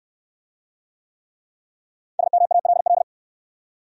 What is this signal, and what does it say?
Morse code, heard as a keyed tone, about 62 wpm, 700 Hz. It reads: SMALL